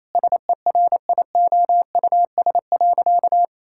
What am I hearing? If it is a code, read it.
Morse code, heard as a keyed tone, about 28 wpm, 700 Hz. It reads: SERIOUS.
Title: Sentences from Top 1600 Words 28wpm